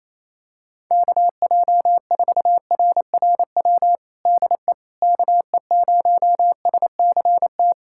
Morse code, heard as a keyed tone, about 28 wpm, 700 Hz. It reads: KJ4RRW DE KE0SCT
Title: USA Callsign DE Callsign - 6 Characters - 28wpm